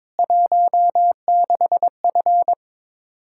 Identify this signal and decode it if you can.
Morse code, heard as a keyed tone, about 22 wpm, 700 Hz. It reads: 16F
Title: Practice QSO - Exchange 2 - Info - 22wpm